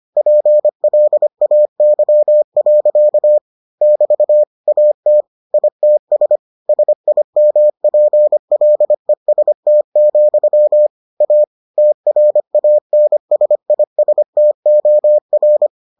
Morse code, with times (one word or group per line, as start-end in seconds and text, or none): 0.2-3.4 PLAY.
3.8-4.4 <BT>
4.7-5.2 AT
5.5-6.4 ITS
6.7-10.9 SIMPLEST,
11.2-11.4 A
11.8-15.7 TRANSISTOR